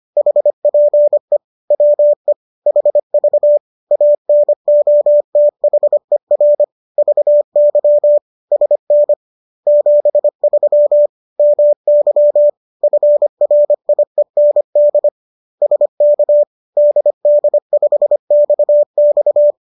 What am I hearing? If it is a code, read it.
Morse code, heard as a keyed tone, about 25 wpm, 590 Hz. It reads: HPE WE HV ANOTHER VY SN 73 MY FRIEND SK DD5XX